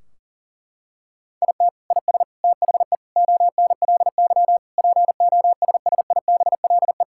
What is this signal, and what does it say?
Morse code, heard as a keyed tone, about 40 wpm, 700 Hz. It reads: IT IS THE ONLY POSSIBLE
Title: Sentences from Top 1600 Words 40wpm